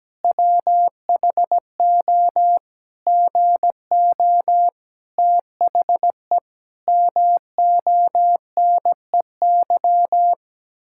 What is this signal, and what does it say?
Morse code, heard as a keyed tone, about 17 wpm, 700 Hz. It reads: WHO GO THE MONEY
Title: Pangram Challenge v2 - 17wpm